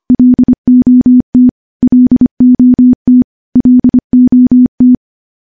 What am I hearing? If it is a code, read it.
Morse code, heard as a keyed tone, about 25 wpm, 250 Hz. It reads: LOT LOT LOT
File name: morse / morse-training-25WPM